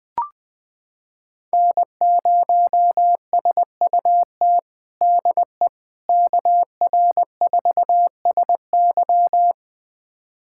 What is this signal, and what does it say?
Morse code, heard as a keyed tone, about 20 wpm, 700 Hz. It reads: N0SUT DE KR4SY
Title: USA Callsign DE Callsign - 5 Characters - 20wpm